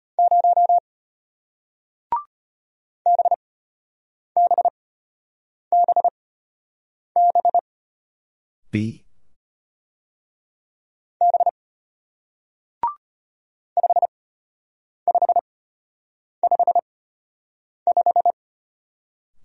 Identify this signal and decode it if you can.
Morse code, heard as a keyed tone, about 33 wpm, spaced as for 8 wpm, 700 Hz. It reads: 0 BBBB B 5555